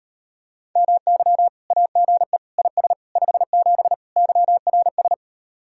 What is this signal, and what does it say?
Morse code, heard as a keyed tone, about 39 wpm, 700 Hz. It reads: MY AGE IS 57 YRS